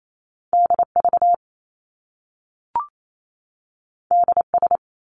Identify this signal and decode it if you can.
Morse code, heard as a keyed tone, about 28 wpm, 700 Hz. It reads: DV DS
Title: Top Two Letter Combinations in Words 28wpm